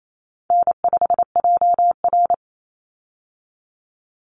Morse code, heard as a keyed tone, about 28 wpm, 700 Hz. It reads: N5JR
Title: Easy Call Signs Repeated 3x - 28wpm